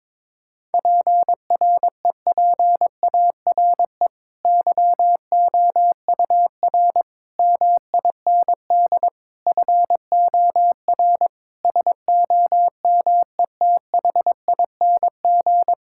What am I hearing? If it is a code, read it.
Morse code, heard as a keyed tone, about 22 wpm, 700 Hz. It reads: PREPARE YOUR MIND FOR SOMETHING
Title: Sentences from Top 1000 Words - Review Part 6 - 22wpm